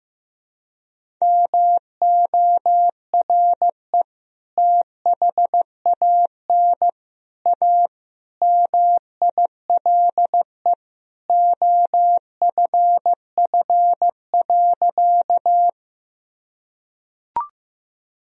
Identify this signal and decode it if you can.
Morse code, heard as a keyed tone, about 15 wpm, 700 Hz. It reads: MORE THAN A MILE OFF.
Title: Sentences from Top 1000 Words - Review Part 1 - 15wpm